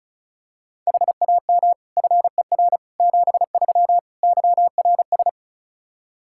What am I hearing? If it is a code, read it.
Morse code, heard as a keyed tone, about 36 wpm, 700 Hz. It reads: HAM FER 73 YRS
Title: Practice QSO - Exchange 3 - More Info - 35wpm